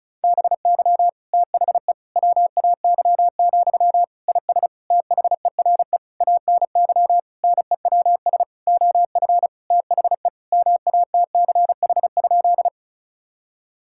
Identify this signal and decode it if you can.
Morse code, heard as a keyed tone, about 36 wpm, 700 Hz. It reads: BY THE WAY, IS THERE ANY NEWS OF THE MATCH?
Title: Sentences from Top 1600 Words 35wpm